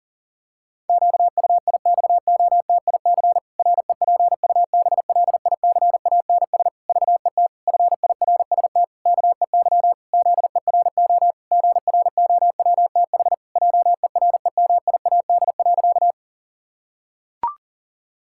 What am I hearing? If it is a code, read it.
Morse code, heard as a keyed tone, about 40 wpm, 700 Hz. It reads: QUIXOTIC REPUBLICANS VET FIRST KEY ZERO GROWTH JEREMIAD.
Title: Pangram Challenge v2 - 40wpm